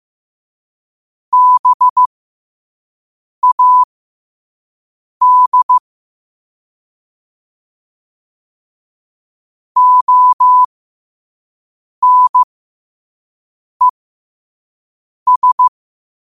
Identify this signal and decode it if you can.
Morse code, heard as a keyed tone, about 15 wpm, spaced as for 5 wpm, 1000 Hz. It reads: BAD ONES
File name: qotdfiles / 5wpm